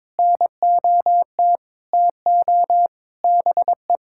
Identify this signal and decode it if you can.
Morse code, heard as a keyed tone, about 22 wpm, 700 Hz. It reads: NOT TO BE
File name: Sentences with 7-Word Phrases - 22wpm